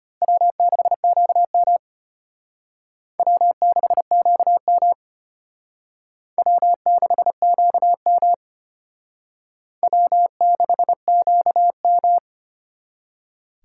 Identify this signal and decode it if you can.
Morse code, heard as a keyed tone, about 33 wpm, 700 Hz. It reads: W6QM W6QM W6QM W6QM